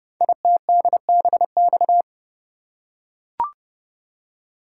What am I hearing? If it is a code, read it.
Morse code, heard as a keyed tone, about 28 wpm, 700 Hz. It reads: ITDBX
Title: Code Groups - 5 Letters - 30wpm